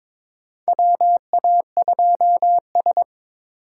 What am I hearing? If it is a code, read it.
Morse code, heard as a keyed tone, about 22 wpm, 700 Hz. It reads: WA2S
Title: USA Call Signs - 4 Characters - 3x Character Spacing - 22wpm